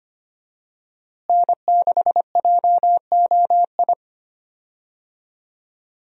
Morse code, heard as a keyed tone, about 25 wpm, 700 Hz. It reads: N6JOI
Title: USA Call Signs - 5 Characters - Repeated 2x - 25wpm